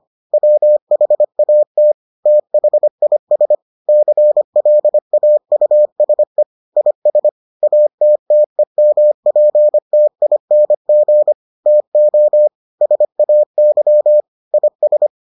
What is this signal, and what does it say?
Morse code, heard as a keyed tone, about 25 wpm, 600 Hz. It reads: WHAT THIS CLAUSE IS ATTEMPTING TO SAY IS